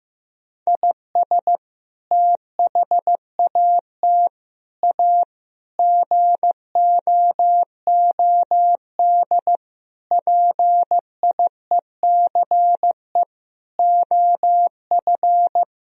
Morse code, heard as a keyed tone, about 14 wpm, 700 Hz. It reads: IS THAT A GOOD PIECE OF
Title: Sentences from Top 1000 Words - Review Part 3 - 15wpm